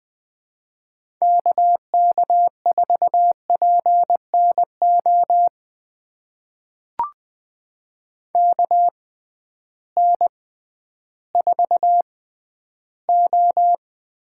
Morse code, heard as a keyed tone, about 20 wpm, spaced as for 12 wpm, 700 Hz. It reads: KK4PNO KN4O
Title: USA Call Signs - 6 Characters - 6x Character Spacing - 20wpm